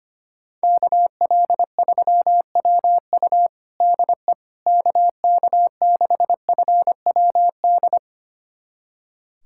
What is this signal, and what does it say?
Morse code, heard as a keyed tone, about 25 wpm, 700 Hz. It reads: KL3WU DE KK6FWD
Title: USA Callsign DE Callsign 25wpm